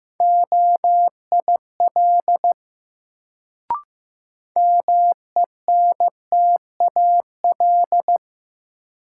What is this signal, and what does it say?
Morse code, heard as a keyed tone, about 15 wpm, 700 Hz. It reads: OIL MENTAL
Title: Sentences from Top 1000 Words - Review Part 4 - New Word Drill - 15wpm